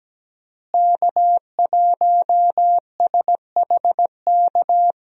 Morse code, heard as a keyed tone, about 17 wpm, 700 Hz. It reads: K1SHK
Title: USA Call Signs - 5 Characters - Repeated 2x - 17wpm